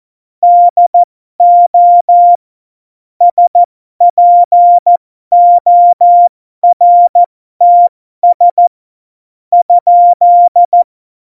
Morse code, heard as a keyed tone, about 14 wpm, spaced as for 12 wpm, 700 Hz. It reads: DO SPORTS ?
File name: Morsekurs / Witze